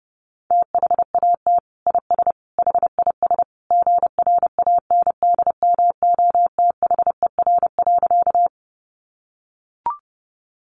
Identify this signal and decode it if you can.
Morse code, heard as a keyed tone, about 30 wpm, 700 Hz. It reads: THAT IS HIS GRANDMOTHER.
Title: Sentences from Top 1200 Words 30wpm